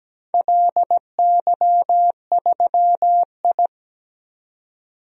Morse code, heard as a keyed tone, about 17 wpm, 700 Hz. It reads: LY3I
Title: DXCC-100 Call Signs - 4 Characters - 17wpm